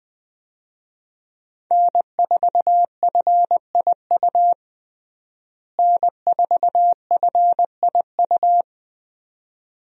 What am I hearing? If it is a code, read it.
Morse code, heard as a keyed tone, about 20 wpm, 700 Hz. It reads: N4FIU N4FIU
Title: USA Call Signs - 5 Characters - Repeated 2x - 20wpm